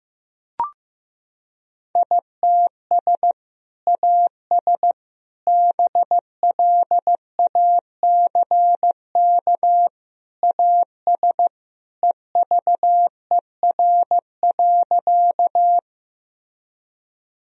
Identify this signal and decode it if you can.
Morse code, heard as a keyed tone, about 15 wpm, 700 Hz. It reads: ITS AS BLACK AS EVER.
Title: Sentences from Top 1000 Words - Review Part 2 - 15wpm